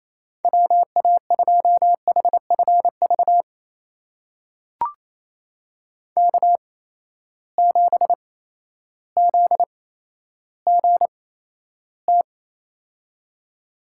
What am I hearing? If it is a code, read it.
Morse code, heard as a keyed tone, about 28 wpm, spaced as for 13 wpm, 700 Hz. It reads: WA2HFV K7ZGT
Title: USA Call Signs - 8x Character Spacing - 28wpm